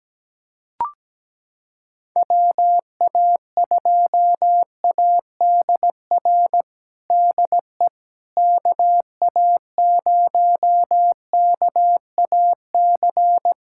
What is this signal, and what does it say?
Morse code, heard as a keyed tone, about 17 wpm, 700 Hz. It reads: WA2ADR DE KA0KAC
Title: USA Callsign DE Callsign - 6 Characters - 17wpm